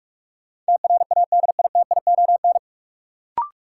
Morse code, heard as a keyed tone, about 46 wpm, 700 Hz. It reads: TRADITION
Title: Sentences from Top 1500 Words - New Word Drill Alt - 45wpm